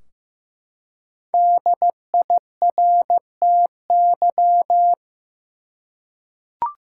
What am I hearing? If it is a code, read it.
Morse code, heard as a keyed tone, about 15 wpm, 700 Hz. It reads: DIRTY
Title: Five Letter Words - 15wpm